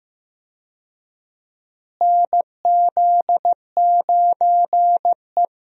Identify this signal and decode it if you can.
Morse code, heard as a keyed tone, about 15 wpm, 700 Hz. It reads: NZ9E
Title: USA Call Signs - 4 Characters - Repeated 3x - 15wpm